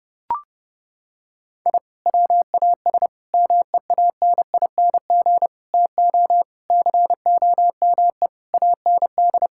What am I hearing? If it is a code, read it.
Morse code, heard as a keyed tone, about 32 wpm, 700 Hz. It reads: I WAS MEANING TO COME AND
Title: Sentences from Top 1000 Words - Review Part 6 - 30wpm